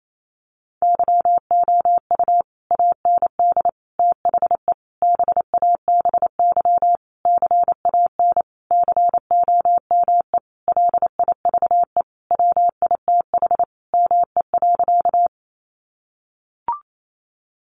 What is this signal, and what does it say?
Morse code, heard as a keyed tone, about 28 wpm, 700 Hz. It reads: YOU AND THE BABY CAN COME LIVE WITH ME.
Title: Sentences from Top 1300 Words 28wpm